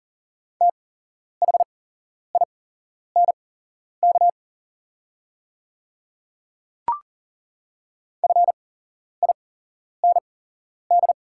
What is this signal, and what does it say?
Morse code, heard as a keyed tone, about 41 wpm, spaced as for 11 wpm, 700 Hz. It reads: THINK FIND
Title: 8x Character Spacing - Top 200 Words - 40wpm